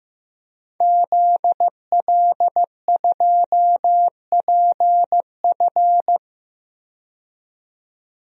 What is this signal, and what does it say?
Morse code, heard as a keyed tone, about 15 wpm, 700 Hz. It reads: ZL2PF
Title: DXCC-100 Call Signs - 5 Characters - 15wpm